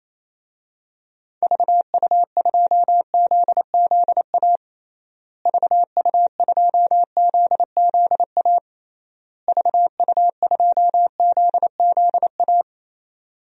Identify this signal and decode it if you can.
Morse code, heard as a keyed tone, about 28 wpm, 700 Hz. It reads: VU2ZZA VU2ZZA VU2ZZA
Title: DXCC-100 Call Signs - 6 Characters - Repeated 3x - 28wpm